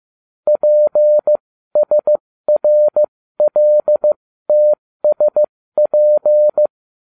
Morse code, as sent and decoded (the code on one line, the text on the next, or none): .--. ... .-. .-.. - ... .--.
PSRLTSP